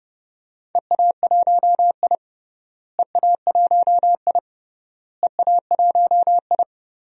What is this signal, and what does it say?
Morse code, heard as a keyed tone, about 30 wpm, 700 Hz. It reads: EA1I EA1I EA1I
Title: DXCC-100 Call Signs - 4 Characters - Repeated 3x - 30wpm